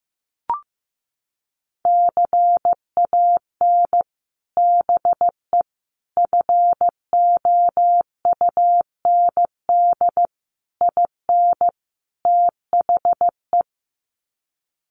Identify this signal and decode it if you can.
Morse code, heard as a keyed tone, about 14 wpm, 700 Hz. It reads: CAN BE FOUND IN THE
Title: Sets of 5 Words Encore 15wpm